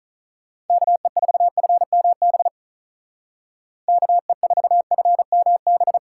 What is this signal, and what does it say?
Morse code, heard as a keyed tone, about 47 wpm, 700 Hz. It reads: KE4FMB KE4FMB